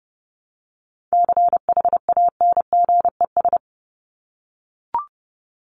Morse code, heard as a keyed tone, about 28 wpm, 700 Hz. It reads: CHANGES
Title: ES:S Suffix - Top 100 Words with Base Words 30wpm